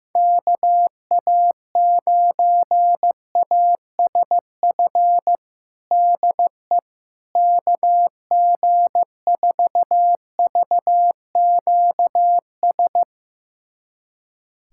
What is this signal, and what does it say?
Morse code, heard as a keyed tone, about 16 wpm, 700 Hz. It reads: KA9ASF DE KG4VQS
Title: USA Callsign DE Callsign 15wpm